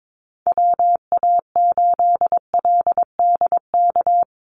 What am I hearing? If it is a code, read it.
Morse code, heard as a keyed tone, about 22 wpm, 700 Hz. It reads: WA8LDK
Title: USA Call Signs 22wpm